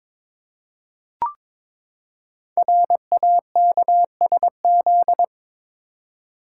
Morse code, heard as a keyed tone, about 22 wpm, 700 Hz. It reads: RAKSZ
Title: Code Groups - 5 Letter:Numbers - 22wpm